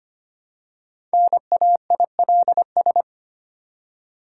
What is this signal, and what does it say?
Morse code, heard as a keyed tone, about 25 wpm, 690 Hz. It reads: NAILS
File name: practice / 25wpm / Easy Words with Hints - 25wpm